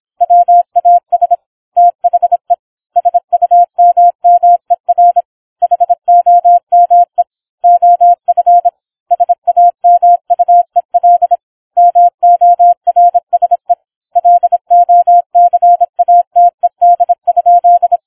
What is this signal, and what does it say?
Morse code, heard as a keyed tone, about 26 wpm, 700 Hz. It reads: WAS THE SUMMER HOME OF SAMUEL MORSE LOCATED?